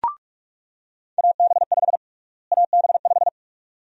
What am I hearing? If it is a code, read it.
Morse code, heard as a keyed tone, about 47 wpm, 700 Hz. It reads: AB5 AB5
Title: USA Call Sign Prefixes - Repeated 2x - 45wpm